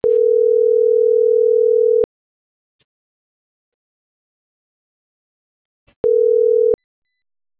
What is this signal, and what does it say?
Ringback tone (North America), 440 + 480 Hz.